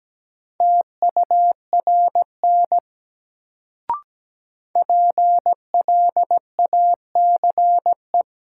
Morse code, heard as a keyed tone, about 17 wpm, 700 Hz. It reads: TURN PLACE
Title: Sentences from Top 200 Words - New Word Drill - 17wpm